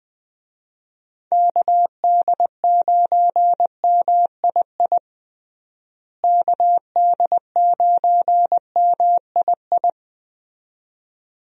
Morse code, heard as a keyed tone, about 20 wpm, 700 Hz. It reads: KD9MII KD9MII
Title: USA Call Signs - 6 Characters - Repeated 2x - 20wpm